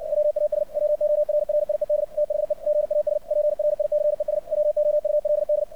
Morse code, heard as a keyed tone, about 56 wpm, 600 Hz. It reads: WAS WORRIED THE GAS PRICES WOULD
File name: archived / CRSnet